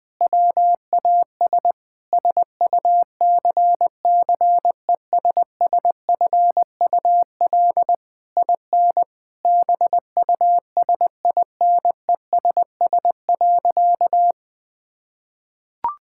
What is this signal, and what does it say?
Morse code, heard as a keyed tone, about 20 wpm, 700 Hz. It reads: WAS SUCCESSFUL IN BUSINESS.
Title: Sentences from Top 1000 Words - Review Part 6 - 20wpm